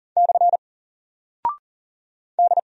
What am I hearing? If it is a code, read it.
Morse code, heard as a keyed tone, about 40 wpm, 700 Hz. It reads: / D
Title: Single Character (Letter:Number:Punctuation:Prosign) Alt 40wpm